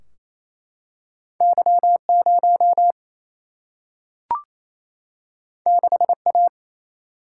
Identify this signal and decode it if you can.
Morse code, heard as a keyed tone, about 28 wpm, 700 Hz. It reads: Y 0 6 A
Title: Single-Letter Single-Number 28wpm